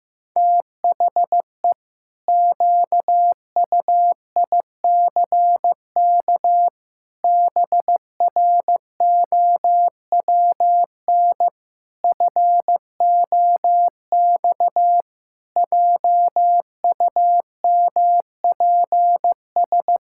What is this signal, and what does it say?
Morse code, heard as a keyed tone, about 16 wpm, 700 Hz. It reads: THE QUICK BROWN FOX JUMPS
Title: Pangram Challenge v2 - 15wpm